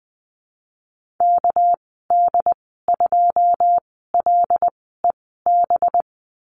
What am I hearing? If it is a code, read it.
Morse code, heard as a keyed tone, about 20 wpm, spaced as for 14 wpm, 700 Hz. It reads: KD2LEB
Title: 231 Lesson 37g - Letter x - Call Signs 20wpm